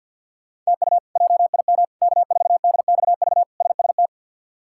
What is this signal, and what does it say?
Morse code, heard as a keyed tone, about 50 wpm, 700 Hz. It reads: TU JIM K4DXV SST
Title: K1USN SST QSO Practice - 50wpm